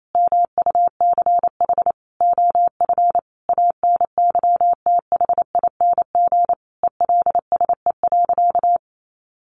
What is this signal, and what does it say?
Morse code, heard as a keyed tone, about 28 wpm, 700 Hz. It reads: MUCH OF ANYTHING ELSE.
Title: Sentences from Top 1500 Words 28wpm